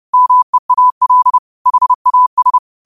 Morse code, heard as a keyed tone, about 30 wpm, 1000 Hz. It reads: MEAL HAS